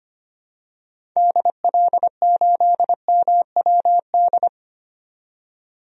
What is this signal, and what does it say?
Morse code, heard as a keyed tone, about 25 wpm, 700 Hz. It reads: DL8MWD
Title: DXCC-100 Call Signs - 6 Characters - 25wpm